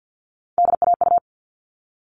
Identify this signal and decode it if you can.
Morse code, heard as a keyed tone, about 50 wpm, 700 Hz. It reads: DAU
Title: All Cut Numbers - 3-Digit 50wpm